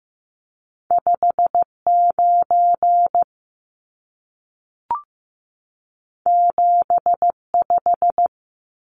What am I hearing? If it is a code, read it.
Morse code, heard as a keyed tone, about 15 wpm, 700 Hz. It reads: 59 75
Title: Numbers - 2-Digit 15wpm